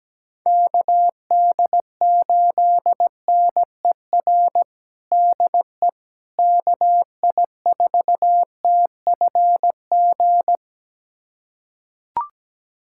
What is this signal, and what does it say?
Morse code, heard as a keyed tone, about 17 wpm, 700 Hz. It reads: KD8NER DE KI4TFG
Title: USA Callsign DE Callsign - 6 Characters - 17wpm